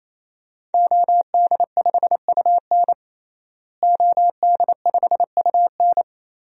Morse code, heard as a keyed tone, about 28 wpm, 700 Hz. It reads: OD5UN OD5UN
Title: DXCC-100 Call Signs - 5 Characters - Repeated 3x - 28wpm